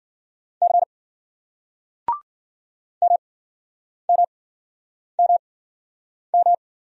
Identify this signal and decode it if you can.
Morse code, heard as a keyed tone, about 53 wpm, spaced as for 9 wpm, 700 Hz. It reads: C MMMM